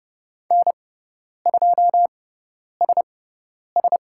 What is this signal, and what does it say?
Morse code, heard as a keyed tone, about 30 wpm, spaced as for 10 wpm, 700 Hz. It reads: N2SS
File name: Lesson 18g - Letter m - Call Signs 30wpm